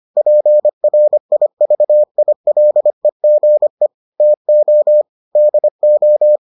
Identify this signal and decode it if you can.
Morse code, heard as a keyed tone, about 25 wpm, 600 Hz. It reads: PRIVILEGE TO DO